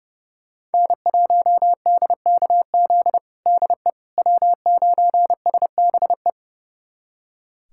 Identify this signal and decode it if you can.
Morse code, heard as a keyed tone, about 30 wpm, 700 Hz. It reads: N1DKZ DE W9SBE
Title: USA Callsign DE Callsign - 5 Characters - 30wpm